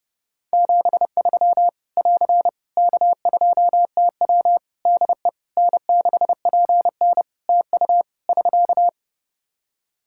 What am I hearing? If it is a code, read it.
Morse code, heard as a keyed tone, about 28 wpm, 700 Hz. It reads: 73 <AR> K2TW DE N6PN TU <SK>
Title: Practice QSO - Exchange 4 - Ending - 30wpm